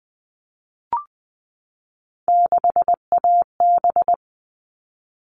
Morse code, heard as a keyed tone, about 20 wpm, 700 Hz. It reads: 6AB
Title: All Cut Numbers - 3-Digit 20wpm